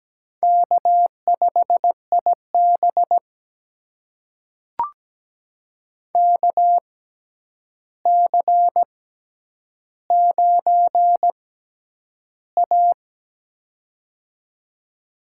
Morse code, heard as a keyed tone, about 17 wpm, spaced as for 9 wpm, 700 Hz. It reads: K5IB KC9A
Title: USA Call Signs - 4 Characters - 6x Character Spacing - 17wpm